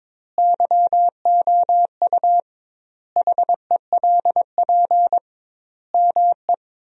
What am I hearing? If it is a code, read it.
Morse code, heard as a keyed tone, about 21 wpm, 700 Hz. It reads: YOU HELP ME
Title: 2x Word-Spacing - Sentences from Top 200 Words - 22wpm